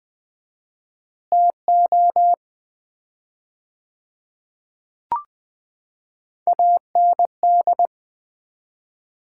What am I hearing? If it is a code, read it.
Morse code, heard as a keyed tone, about 20 wpm, 700 Hz. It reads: TO AND
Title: 4x Word-Spacing - Sets of 5 Words - 20wpm